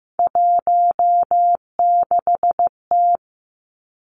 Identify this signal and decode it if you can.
Morse code, heard as a keyed tone, about 15 wpm, 700 Hz. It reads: 16T
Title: Three Challenging Letter-Number Combinations 15wpm